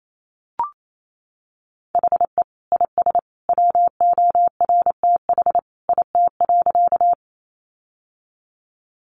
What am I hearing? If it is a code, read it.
Morse code, heard as a keyed tone, about 28 wpm, 700 Hz. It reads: HE IS WORTH IT.
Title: Sentences from Top 800 Words 28wpm